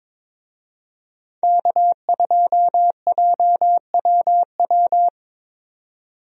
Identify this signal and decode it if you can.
Morse code, heard as a keyed tone, about 21 wpm, 700 Hz. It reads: K2JWW
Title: USA Call Signs - 5 Characters - 22wpm